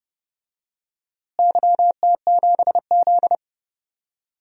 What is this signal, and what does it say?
Morse code, heard as a keyed tone, about 31 wpm, 700 Hz. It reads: YT7Z
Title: DXCC-100 Call Signs - 4 Characters - Repeated 2x - 30wpm